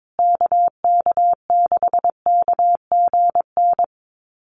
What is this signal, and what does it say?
Morse code, heard as a keyed tone, about 22 wpm, 700 Hz. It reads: KK6KGN
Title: USA Call Signs Repeated 3x 22wpm